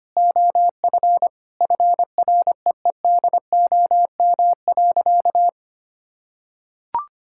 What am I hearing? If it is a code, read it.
Morse code, heard as a keyed tone, about 25 wpm, 700 Hz. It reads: OF FREEDOM.
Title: Sentences from Top 1600 Words 25wpm